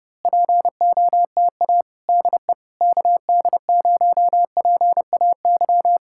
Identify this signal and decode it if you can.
Morse code, heard as a keyed tone, about 29 wpm, 700 Hz. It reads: POTA DE KD0PAY
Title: Speed-Racing Common QSO Phrases - 30wpm